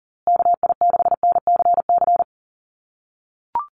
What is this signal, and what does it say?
Morse code, heard as a keyed tone, about 40 wpm, 700 Hz. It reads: KI6NCC
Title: USA Call Signs Repeated 3x 40wpm